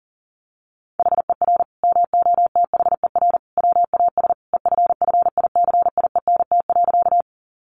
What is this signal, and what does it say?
Morse code, heard as a keyed tone, about 37 wpm, 700 Hz. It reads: HER MOTHER WAS EFFICIENT.
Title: Sentences from Top 1300 Words 40wpm